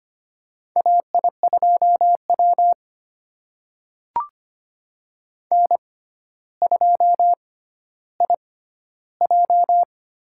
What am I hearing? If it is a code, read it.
Morse code, heard as a keyed tone, about 25 wpm, spaced as for 13 wpm, 700 Hz. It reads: AI2W N2IJ
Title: USA Call Signs - 6x Character Spacing - 25wpm